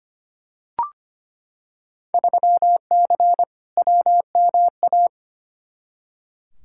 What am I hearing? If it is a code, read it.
Morse code, heard as a keyed tone, about 25 wpm, 700 Hz. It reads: 3C WMA
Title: Field Day Exchanges Encore - 25wpm